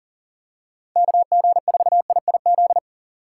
Morse code, heard as a keyed tone, about 40 wpm, 700 Hz. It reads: KG4IIZ
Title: USA Call Signs - 6 Characters - 6x Character Spacing - 40wpm